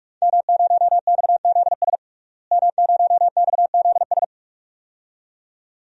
Morse code, heard as a keyed tone, about 41 wpm, 700 Hz. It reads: M0XZS M0XZS
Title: DXCC-100 Call Signs - 5 Characters - Repeated 2x - 45wpm